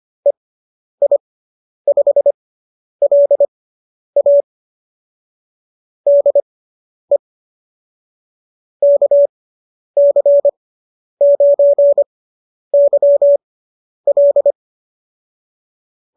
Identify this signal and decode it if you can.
Morse code, heard as a keyed tone, about 25 wpm, spaced as for 10 wpm, 570 Hz. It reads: EI5LA DE KC9YL